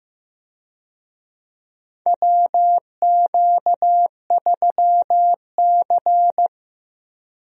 Morse code, heard as a keyed tone, about 16 wpm, 700 Hz. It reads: WQ3C
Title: USA Call Signs - 4 Characters - 15wpm